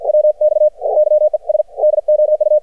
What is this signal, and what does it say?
Morse code, heard as a keyed tone, about 55 wpm, 610 Hz. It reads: WX YES DOU
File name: archived / CRSnet